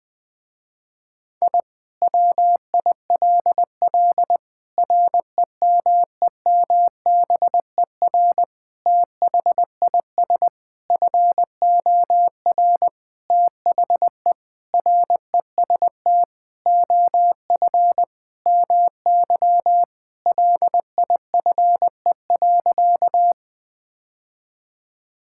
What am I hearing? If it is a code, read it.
Morse code, heard as a keyed tone, about 20 wpm, 700 Hz. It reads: I WILL REMEMBER THIS FOR THE REST OF MY LIFE.
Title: Sentences with 5-Word Phrases - Spoken Beginning - 20wpm